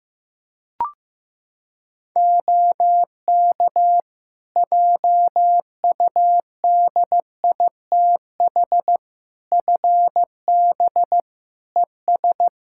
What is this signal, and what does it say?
Morse code, heard as a keyed tone, about 15 wpm, 700 Hz. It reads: OK JUDITH FB ES
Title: Practice QSO - Exchange 2 - Info - 15wpm